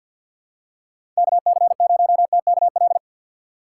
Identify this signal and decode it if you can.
Morse code, heard as a keyed tone, about 51 wpm, 700 Hz. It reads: KC0TKL
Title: USA Call Signs - 6 Characters - 3x Character Spacing - 50wpm